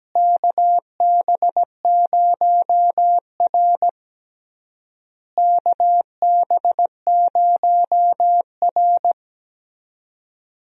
Morse code, heard as a keyed tone, about 17 wpm, 700 Hz. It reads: KB0R KB0R
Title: USA Call Signs - 4 Characters - Repeated 3x - 17wpm